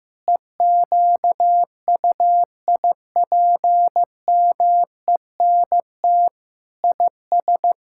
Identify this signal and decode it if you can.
Morse code, heard as a keyed tone, about 15 wpm, 700 Hz. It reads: EQUIPMENT IS
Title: Sentences from Top 1000 Words - Review Part 5 - 15wpm